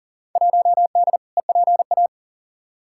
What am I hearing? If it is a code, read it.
Morse code, heard as a keyed tone, about 40 wpm, 700 Hz. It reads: 1D EPA
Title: Field Day Exchanges - 40wpm